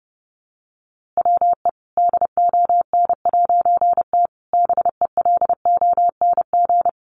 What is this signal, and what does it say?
Morse code, heard as a keyed tone, about 30 wpm, 700 Hz. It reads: WE DON'T BELONG
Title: A Sound Of Thunder 30wpm